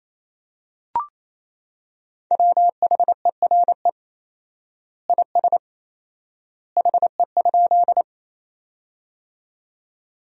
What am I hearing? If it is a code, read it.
Morse code, heard as a keyed tone, about 28 wpm, 700 Hz. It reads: WHERE IS HE?
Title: 4x Word-Spacing - Sentences from Top 200 Words - 28wpm